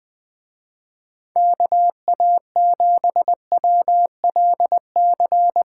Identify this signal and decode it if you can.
Morse code, heard as a keyed tone, about 22 wpm, 700 Hz. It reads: KA7WLC
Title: Speed Racing - USA 2x3 Call Signs 20wpm